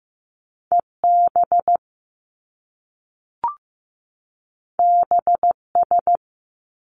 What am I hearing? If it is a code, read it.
Morse code, heard as a keyed tone, about 15 wpm, 700 Hz. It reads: EB BS
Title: Top Two Letter Combinations in Words 15wpm